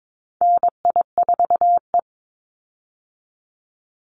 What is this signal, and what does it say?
Morse code, heard as a keyed tone, about 22 wpm, 700 Hz. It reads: NI4E
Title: Call Signs Repeated 3x - 22wpm